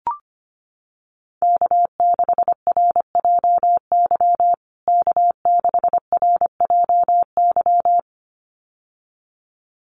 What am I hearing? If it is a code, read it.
Morse code, heard as a keyed tone, about 25 wpm, 700 Hz. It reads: K6RJY K6RJY
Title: USA Call Signs Repeated 2x 25wpm